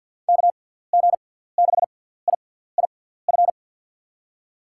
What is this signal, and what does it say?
Morse code, heard as a keyed tone, about 47 wpm, spaced as for 17 wpm, 700 Hz. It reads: KG6IIF
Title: USA Call Signs - 6 Characters - 6x Character Spacing - 50wpm